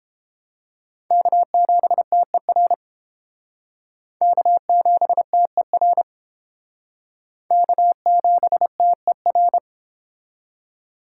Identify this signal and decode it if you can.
Morse code, heard as a keyed tone, about 29 wpm, 700 Hz. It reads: K7TER K7TER K7TER